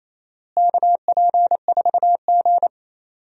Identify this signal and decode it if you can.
Morse code, heard as a keyed tone, about 28 wpm, 700 Hz. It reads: KP4G
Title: DXCC-100 Call Signs - 4 Characters - Repeated 2x - 28wpm